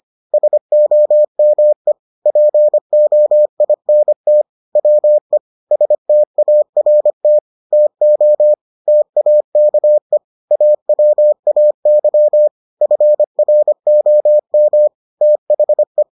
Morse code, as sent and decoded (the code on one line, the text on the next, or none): ... --- -- . / .--. --- .. -. - / .-- . / ... - .- .-. - / - --- / - .- -.- . / .- .-- .- -.-- / ..-. .-. --- -- / - .... .
SOME POINT WE START TO TAKE AWAY FROM THE